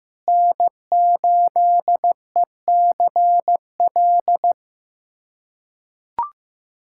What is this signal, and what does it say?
Morse code, heard as a keyed tone, about 15 wpm, 700 Hz. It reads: N8ECL
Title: USA Callsign DE Callsign - 5 Characters - 15wpm